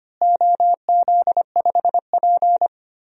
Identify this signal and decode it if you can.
Morse code, heard as a keyed tone, about 25 wpm, 700 Hz. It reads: OZ5P
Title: DXCC-100 Call Signs - 4 Characters - Repeated 2x - 25wpm